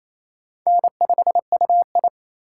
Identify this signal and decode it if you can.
Morse code, heard as a keyed tone, about 28 wpm, 700 Hz. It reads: N5UI
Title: K1USN SST QSO Practice - 28wpm